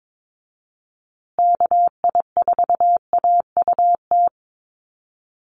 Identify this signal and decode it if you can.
Morse code, heard as a keyed tone, about 22 wpm, 700 Hz. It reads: KI4AUT
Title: USA Call Signs Repeated 2x 22wpm